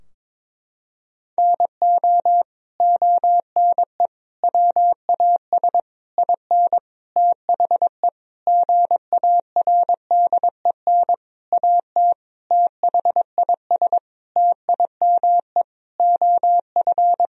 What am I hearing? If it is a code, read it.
Morse code, heard as a keyed tone, about 22 wpm, 700 Hz. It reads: NO ONE WAS IN THE GARDEN AT THIS TIME OF
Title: Sentences from Top 1000 Words - Review Part 4 - 22wpm